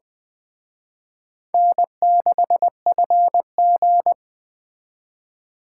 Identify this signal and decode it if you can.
Morse code, heard as a keyed tone, about 23 wpm, 700 Hz. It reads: N6FG
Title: Speed Racing - USA 1x2 Call Signs 20wpm